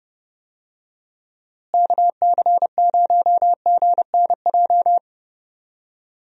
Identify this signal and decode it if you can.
Morse code, heard as a keyed tone, about 30 wpm, 700 Hz. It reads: KC0GNJ
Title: USA Call Signs - 6 Characters - Repeated 3x - 30wpm